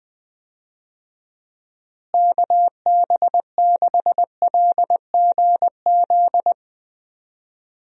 Morse code, heard as a keyed tone, about 21 wpm, 700 Hz. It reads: KB6LGZ
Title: USA Call Signs - 6 Characters - Repeated 2x - 20wpm